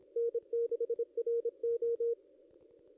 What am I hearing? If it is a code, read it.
Morse code, heard as a keyed tone, about 25 wpm, 470 Hz. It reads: N6RO
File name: contests / ss-cw-k5tr-op-n5rz -0800